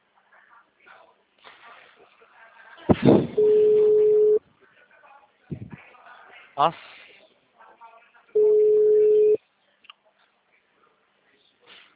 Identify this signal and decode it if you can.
Ringback tone (Europe, 425 Hz), with speech.